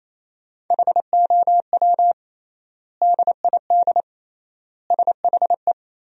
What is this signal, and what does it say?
Morse code, heard as a keyed tone, about 27 wpm, 700 Hz. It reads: HOW DID SHE
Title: 3x Word-Spacing - Sentences from Top 100 Words - 28wpm